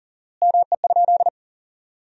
Morse code, heard as a keyed tone, about 40 wpm, 700 Hz. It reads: ME?
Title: 4x Word-Spacing - Sentences from Top 200 Words - 40wpm